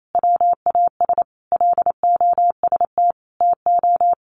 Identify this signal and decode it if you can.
Morse code, heard as a keyed tone, about 28 wpm, 700 Hz. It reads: WAS LOST TO
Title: Sentences from Top 1400 Words 28wpm